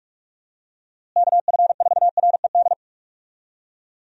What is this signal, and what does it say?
Morse code, heard as a keyed tone, about 46 wpm, 700 Hz. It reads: KF4RED
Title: USA Call Signs - 6 Characters - Repeated 3x - 45wpm